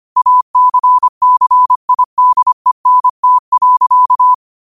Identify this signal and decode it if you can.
Morse code, heard as a keyed tone, about 25 wpm, 1000 Hz. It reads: ACCIDENT.
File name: qotdfiles / 25wpm